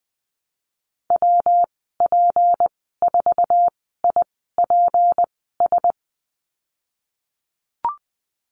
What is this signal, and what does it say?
Morse code, heard as a keyed tone, about 20 wpm, spaced as for 15 wpm, 700 Hz. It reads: WP4IPS
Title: 231 Lesson 37g - Letter x - Call Signs 20wpm